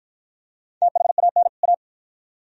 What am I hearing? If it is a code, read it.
Morse code, heard as a keyed tone, about 53 wpm, 700 Hz. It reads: THAN A